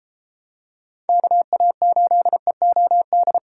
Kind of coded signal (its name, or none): Morse code